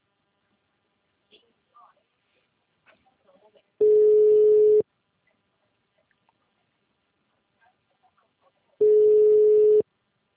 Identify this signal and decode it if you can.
Ringback tone (Europe, 425 Hz), with speech.